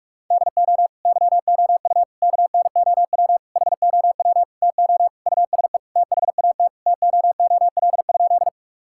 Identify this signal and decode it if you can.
Morse code, heard as a keyed tone, about 46 wpm, 690 Hz. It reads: DO YOU KNOW HOW TO USE THAT TOOL?
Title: Sentences from Top 1000 Words - Review Part 6 - 45wpm